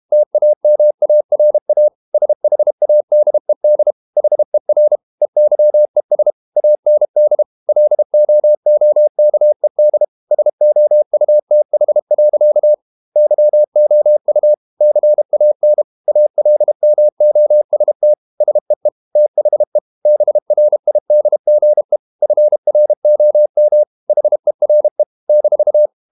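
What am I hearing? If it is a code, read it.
Morse code, heard as a keyed tone, about 32 wpm, 600 Hz. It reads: TAMARA SHADED HER EYES AND LOOKED SOUTH. YOU CAN ALMOST SEE THE BRIDGE FROM HERE -
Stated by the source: Morse code practice at 32 wpm from CW med Gnister, reading Ken Follett's novel Never in English.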